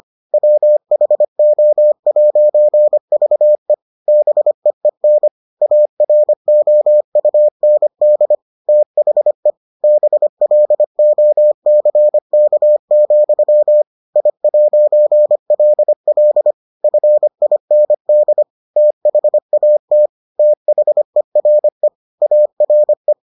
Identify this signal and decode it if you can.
Morse code, heard as a keyed tone, about 25 wpm, 600 Hz. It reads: WHO'VE BEEN AROUND THE BLOCK, I'LL FIND THAT THERE ARE